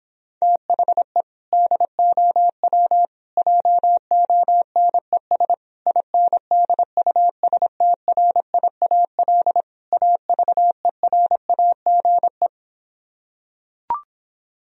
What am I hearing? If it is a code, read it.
Morse code, heard as a keyed tone, about 26 wpm, 700 Hz. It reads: THE DOW JONES INDUSTRIAL AVERAGE